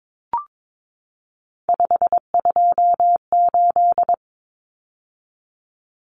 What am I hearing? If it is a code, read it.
Morse code, heard as a keyed tone, about 22 wpm, 700 Hz. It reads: 528
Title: Numbers - 3-Digit 22wpm